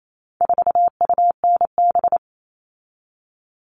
Morse code, heard as a keyed tone, about 28 wpm, 700 Hz. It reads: 4UNB
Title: All Cut Numbers - 4-Digit 28wpm